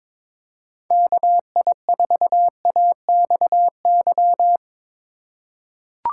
Morse code, heard as a keyed tone, about 22 wpm, 700 Hz. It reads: KI4AXY
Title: USA Call Signs - 6 Characters - 2x Character Spacing - 22wpm